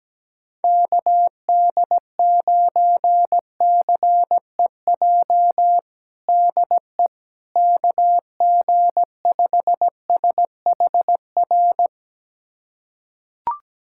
Morse code, heard as a keyed tone, about 17 wpm, 700 Hz. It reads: KD9CEJ DE KG5SHR
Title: USA Callsign DE Callsign - 6 Characters - 17wpm